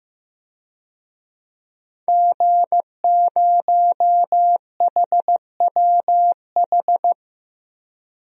Morse code, heard as a keyed tone, about 15 wpm, 700 Hz. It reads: G0HWH
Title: DXCC-100 Call Signs - 5 Characters - Repeated 3x - 15wpm